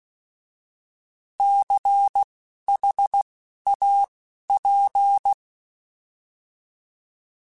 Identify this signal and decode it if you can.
Morse code, heard as a keyed tone, about 16 wpm, spaced as for 11 wpm, 800 Hz. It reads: CHAP
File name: Alphabetical / Alphabetical mp3s / Cs @10